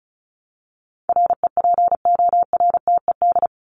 Morse code, heard as a keyed tone, about 34 wpm, 700 Hz. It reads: REPORTED
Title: ED Suffix - Top 100 Words with Base Words 35wpm